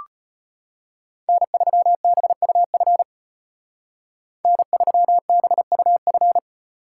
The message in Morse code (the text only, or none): N3BUF N3BUF